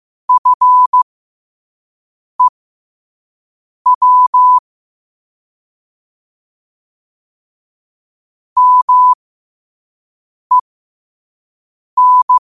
Morse code, heard as a keyed tone, about 15 wpm, spaced as for 5 wpm, 1000 Hz. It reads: FEW MEN
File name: qotdfiles / 5wpm